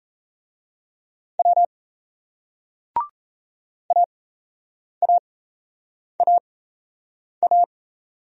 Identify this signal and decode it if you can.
Morse code, heard as a keyed tone, about 35 wpm, spaced as for 8 wpm, 700 Hz. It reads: W AAAA